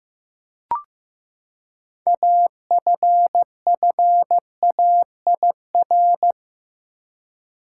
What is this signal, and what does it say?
Morse code, heard as a keyed tone, about 15 wpm, 700 Hz. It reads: AFFAIR
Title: Six Letter Words - 15wpm